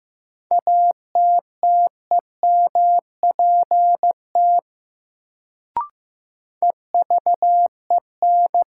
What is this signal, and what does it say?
Morse code, heard as a keyed tone, about 15 wpm, 700 Hz. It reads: ATTEMPT EVEN
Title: Sets of 4 Words - New Word Drill - 15wpm